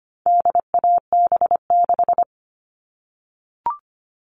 Morse code, heard as a keyed tone, about 25 wpm, 700 Hz. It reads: DAB6
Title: All Cut Numbers - 4-Digit 25wpm